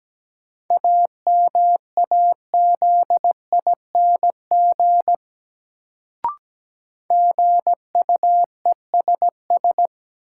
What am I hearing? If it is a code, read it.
Morse code, heard as a keyed tone, about 17 wpm, 700 Hz. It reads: AMAZING GUESS
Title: Sentences from Top 1500 Words - New Word Drill - 17wpm